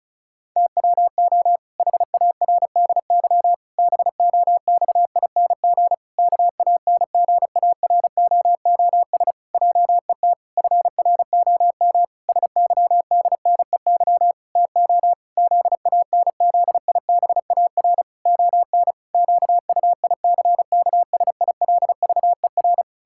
Morse code, heard as a keyed tone, about 36 wpm, 700 Hz. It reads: TWO HARDY BOXING KANGAROOS JET FROM SYDNEY TO ZANZIBAR ON QUICKSILVER
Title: Pangram Challenge v2 - 35wpm